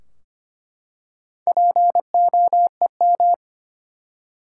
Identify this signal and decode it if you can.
Morse code, heard as a keyed tone, about 25 wpm, 700 Hz. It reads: POEM
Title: Four Letter Words - 25wpm